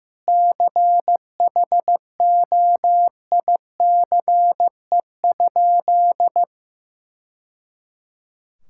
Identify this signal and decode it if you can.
Morse code, heard as a keyed tone, about 16 wpm, 700 Hz. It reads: CHOICE?
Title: Sentences from Top 1000 Words - Review Part 6 - 15wpm